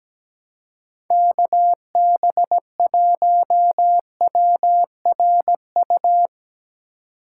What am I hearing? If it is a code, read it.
Morse code, heard as a keyed tone, about 17 wpm, 700 Hz. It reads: KB1WRU
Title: USA Call Signs - 6 Characters - 4x Character Spacing - 17wpm